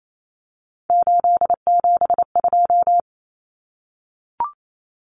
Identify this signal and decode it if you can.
Morse code, heard as a keyed tone, about 28 wpm, 700 Hz. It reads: 872
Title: Common Cut Numbers - 3-Digit 28wpm